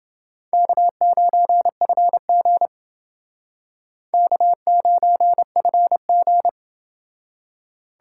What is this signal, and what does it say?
Morse code, heard as a keyed tone, about 29 wpm, 700 Hz. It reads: K9FG K9FG